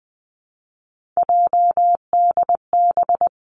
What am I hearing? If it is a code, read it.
Morse code, heard as a keyed tone, about 20 wpm, 700 Hz. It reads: JDB
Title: Three Challenging Letter-Number Combinations 20wpm